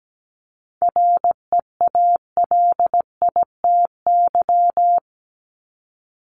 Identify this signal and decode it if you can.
Morse code, heard as a keyed tone, about 17 wpm, 700 Hz. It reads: REALITY
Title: Top 1001-1250 Words 17wpm